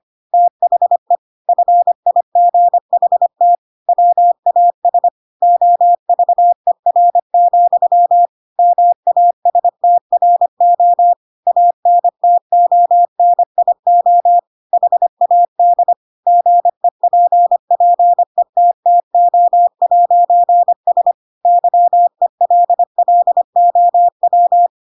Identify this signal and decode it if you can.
Morse code, heard as a keyed tone, about 25 wpm, 700 Hz. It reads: THE FIGHT WAS OVER, MASTRO ANTONIO HAD GEPPETTO'S YELLOW
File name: Pinocchio / Pinocchio ch2 25 25 ik1hge